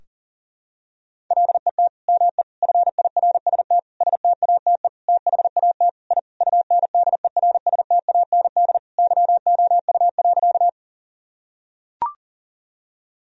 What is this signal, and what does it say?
Morse code, heard as a keyed tone, about 40 wpm, 700 Hz. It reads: LET ME FIRST STATE THAT I UNDERSTAND YOU.
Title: Sentences from Top 1000 Words - Review Part 2 - 40wpm